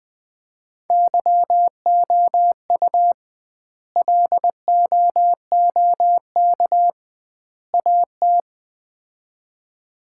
Morse code, heard as a keyed tone, about 21 wpm, 700 Hz. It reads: YOU LOOK AT
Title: 2x Word-Spacing - Sets of 3 Words - 20wpm